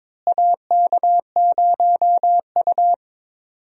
Morse code, heard as a keyed tone, about 22 wpm, 700 Hz. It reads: AK0U
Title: USA Call Signs - 4 Characters - Repeated 3x - 22wpm